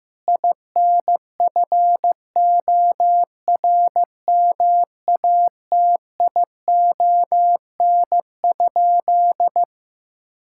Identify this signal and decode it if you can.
Morse code, heard as a keyed tone, about 15 wpm, 700 Hz. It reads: INFORMATION?
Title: Sentences from Top 1000 Words - Review Part 6 - 15wpm